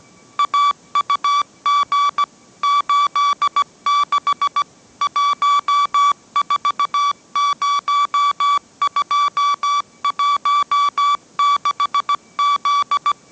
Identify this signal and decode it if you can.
Morse code, heard as a keyed tone, about 17 wpm, 1150 Hz. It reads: AUG 86 140216Z
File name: mrx / mrx080 18wpm